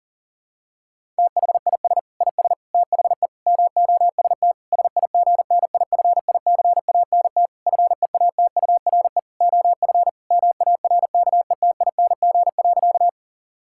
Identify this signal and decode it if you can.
Morse code, heard as a keyed tone, about 40 wpm, 700 Hz. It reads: THIS IS THE MOST SIGNIFICANT FEATURE OF MARKETING.
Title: Sentences from Top 1000 Words - Review Part 6 - 40wpm